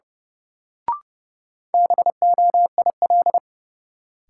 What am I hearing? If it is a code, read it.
Morse code, heard as a keyed tone, about 30 wpm, 700 Hz. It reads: BOIL
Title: Sets of 5 Words Encore - New Word Drill - 30wpm